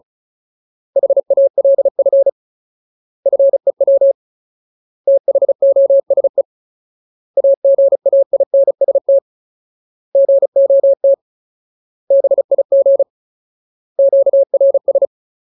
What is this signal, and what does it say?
Morse code, heard as a keyed tone, about 34 wpm, 550 Hz. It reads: HALF FEW THOSE AGAINST GOT BIG QRS
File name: Ultimate Word List / 35WPM Ultimate